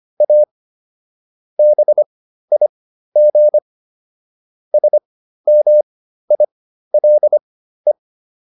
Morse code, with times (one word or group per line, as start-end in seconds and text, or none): 0.2-0.4 A
1.6-3.6 BIG
4.7-7.9 SMILE